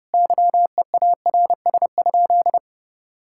Morse code, heard as a keyed tone, about 32 wpm, 700 Hz. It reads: YEARS?
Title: Sentences from Top 1000 Words - Review Part 6 - 30wpm